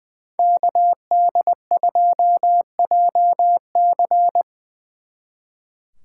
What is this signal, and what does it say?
Morse code, heard as a keyed tone, about 22 wpm, 700 Hz. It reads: KD2JC
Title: Speed Racing - USA 2x2 Call Signs 20wpm